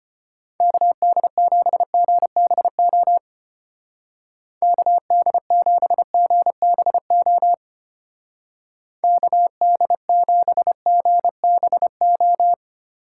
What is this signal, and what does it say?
Morse code, heard as a keyed tone, about 30 wpm, 700 Hz. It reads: KD7GBO KD7GBO KD7GBO
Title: Speed Racing - USA 2x3 Call Signs 25wpm